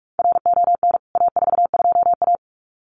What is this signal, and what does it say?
Morse code, heard as a keyed tone, about 44 wpm, 700 Hz. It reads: RON A42A
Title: CWT Exchange 45wpm